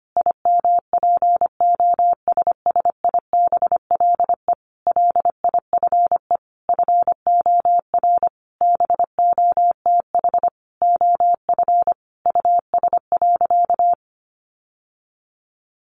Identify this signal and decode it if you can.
Morse code, heard as a keyed tone, about 25 wpm, 700 Hz. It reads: IMPOSSIBLE LIFE FOR BOTH OF US.
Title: Sentences from Top 1100 Words 25wpm